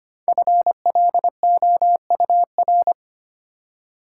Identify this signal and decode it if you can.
Morse code, heard as a keyed tone, about 25 wpm, 700 Hz. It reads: FLOUR
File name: 5 Letter Words - First 3 Letters Given - 25wpm